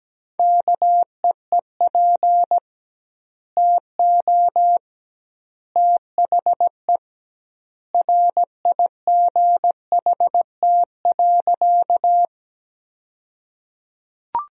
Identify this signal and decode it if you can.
Morse code, heard as a keyed tone, about 17 wpm, 700 Hz. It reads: KEEP TO THE RIGHT.
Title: 2x Word-Spacing - Sentences from Top 300 Words - 17wpm